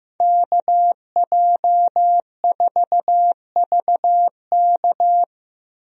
Morse code, heard as a keyed tone, about 15 wpm, 700 Hz. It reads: KJ4VK
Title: USA Call Signs - 5 Characters - Repeated 3x - 15wpm